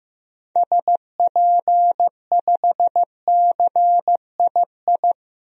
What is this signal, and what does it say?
Morse code, heard as a keyed tone, about 15 wpm, 700 Hz. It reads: SP5CII
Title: DXCC-100 Call Signs - 6 Characters - 15wpm